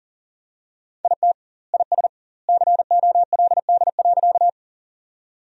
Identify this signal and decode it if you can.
Morse code, heard as a keyed tone, about 40 wpm, 700 Hz. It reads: IT IS COLD.
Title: 2x Word-Spacing - Sentences from Top 400 Words - 40wpm